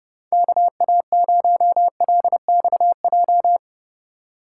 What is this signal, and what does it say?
Morse code, heard as a keyed tone, about 30 wpm, 700 Hz. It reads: KA0LXJ